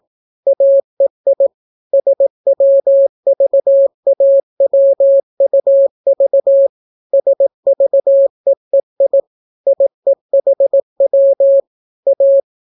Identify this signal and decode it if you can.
Morse code, heard as a keyed tone, about 18 wpm, 550 Hz. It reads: AEI SWVAWUV SVEEI IEHW A